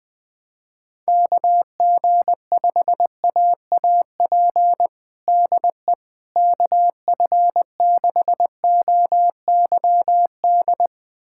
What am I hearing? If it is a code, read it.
Morse code, heard as a keyed tone, about 20 wpm, 700 Hz. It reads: KG5AAP DE KF6OYD
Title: USA Callsign DE Callsign - 6 Characters - 20wpm